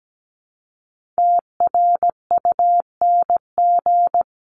Morse code, heard as a keyed tone, about 17 wpm, 700 Hz. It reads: TRUNG
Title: CWT Exchange 17wpm